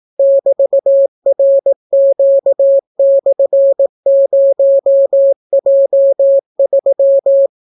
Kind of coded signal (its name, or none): Morse code